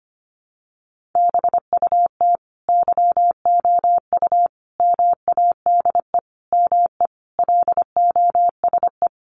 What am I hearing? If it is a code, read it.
Morse code, heard as a keyed tone, about 25 wpm, 700 Hz. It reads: BUT YOU MADE ME LOSE
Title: Sentences from Top 1300 Words 25wpm